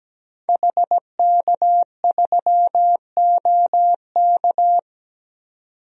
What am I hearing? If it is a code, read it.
Morse code, heard as a keyed tone, about 17 wpm, 700 Hz. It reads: HK3OK
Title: DXCC-100 Call Signs - 5 Characters - 17wpm